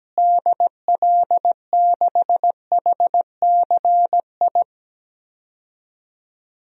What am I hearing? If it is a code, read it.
Morse code, heard as a keyed tone, about 17 wpm, 700 Hz. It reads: DL6HCI
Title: DXCC-100 Call Signs - 6 Characters - Repeated 3x - 17wpm